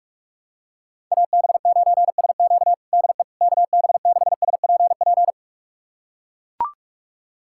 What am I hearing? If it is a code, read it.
Morse code, heard as a keyed tone, about 44 wpm, 690 Hz. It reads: AB9SQ DE KB6SPP
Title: USA Callsign DE Callsign 45wpm